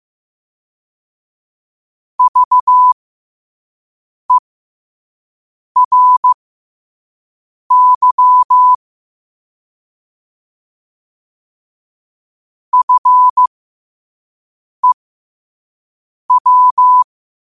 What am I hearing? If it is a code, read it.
Morse code, heard as a keyed tone, about 15 wpm, spaced as for 5 wpm, 1000 Hz. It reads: VERY FEW
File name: qotdfiles / 5wpm